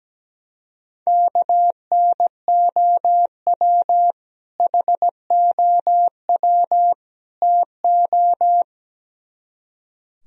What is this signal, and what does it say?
Morse code, heard as a keyed tone, about 17 wpm, 700 Hz. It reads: KNOW HOW TO
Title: Speed-Racing - Sets of 3 Words 17wpm